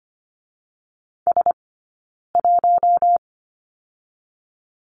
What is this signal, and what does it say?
Morse code, heard as a keyed tone, about 25 wpm, 700 Hz. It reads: S 1
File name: Lesson 15d - Number 5 - Two Characters 25wpm